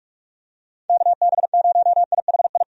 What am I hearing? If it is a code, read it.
Morse code, heard as a keyed tone, about 44 wpm, 700 Hz. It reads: KB0IHI
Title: USA Call Signs - 6 Characters - 45wpm